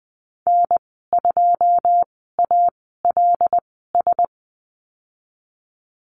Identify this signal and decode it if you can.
Morse code, heard as a keyed tone, about 20 wpm, spaced as for 15 wpm, 700 Hz. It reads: N2ALS
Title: 122 Lesson 20g - Number 3 - Call Signs 20wpm